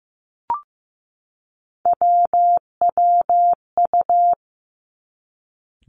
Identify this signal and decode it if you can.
Morse code, heard as a keyed tone, about 15 wpm, 700 Hz. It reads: WWU
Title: Three Challenging Letter-Number Combinations 15wpm